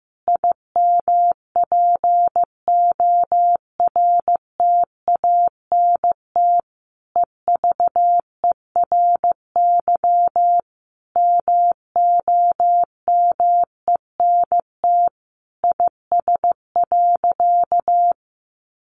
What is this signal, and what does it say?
Morse code, heard as a keyed tone, about 15 wpm, 700 Hz. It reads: IMPORTANT EVERY MOMENT IS.
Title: Sentences from Top 1000 Words 15wpm